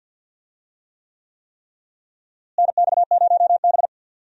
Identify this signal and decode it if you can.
Morse code, heard as a keyed tone, about 52 wpm, 700 Hz. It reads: NX0B
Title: USA Call Signs - 4 Characters - Repeated 3x - 50wpm